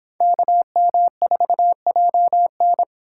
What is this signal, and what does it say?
Morse code, heard as a keyed tone, about 31 wpm, 700 Hz. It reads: KM4JN